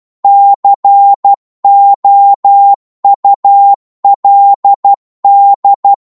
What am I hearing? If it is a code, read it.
Morse code, heard as a keyed tone, about 12 wpm, 800 Hz. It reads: COULD